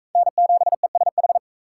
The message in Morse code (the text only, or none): N7ESH